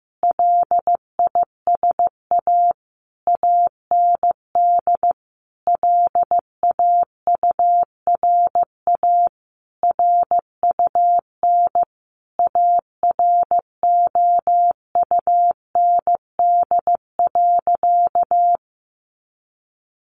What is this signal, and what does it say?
Morse code, heard as a keyed tone, about 15 wpm, 700 Hz. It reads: LISA AND LAURA RUN AROUND.
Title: Additional Easy Phrases 15wpm